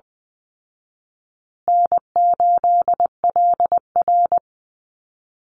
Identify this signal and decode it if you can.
Morse code, heard as a keyed tone, about 20 wpm, 700 Hz. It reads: N8LR
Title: Easy Call Signs Repeated 3x - 20wpm